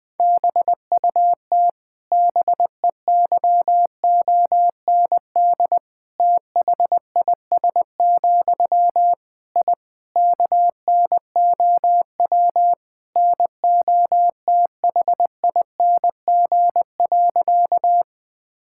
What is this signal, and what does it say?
Morse code, heard as a keyed tone, about 20 wpm, 700 Hz. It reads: BUT BEYOND THIS, I KNOW NOTHING.
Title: Sentences from Top 1000 Words - Review Part 6 - 20wpm